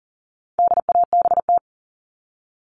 Morse code, heard as a keyed tone, about 40 wpm, 700 Hz. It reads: DABT
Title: All Cut Numbers - 4-Digit 40wpm